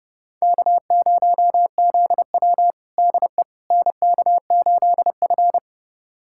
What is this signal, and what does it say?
Morse code, heard as a keyed tone, about 32 wpm, 700 Hz. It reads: K0ZW DE NK8F
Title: USA Callsign DE Callsign - 4 Characters - 30wpm